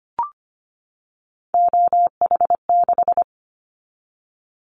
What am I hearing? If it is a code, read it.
Morse code, heard as a keyed tone, about 25 wpm, 700 Hz. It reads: OH6
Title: Callsign Prefixes for Top 50 DXCC - 25wpm